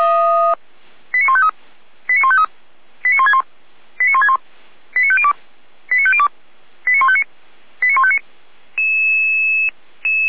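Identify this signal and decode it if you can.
Five-tone selective call (ZVEI), heard as quick runs of five tones, tones of about 70 ms. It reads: 89153 89153 89161 89161 89502 89502 89159 89159
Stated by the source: SDR receiver, ZVEI five-tone selective call, address 89160, feuerwehr-orlamuende (fire brigade Orlamünde).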